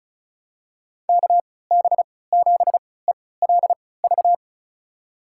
Morse code, heard as a keyed tone, about 36 wpm, spaced as for 20 wpm, 700 Hz. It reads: KB7ELV
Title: USA Call Signs - 6 Characters - 3x Character Spacing - 35wpm